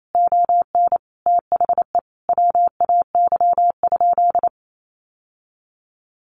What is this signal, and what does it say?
Morse code, heard as a keyed tone, about 28 wpm, 700 Hz. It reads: ON THE WAY?
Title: Sentences from Top 1100 Words 28wpm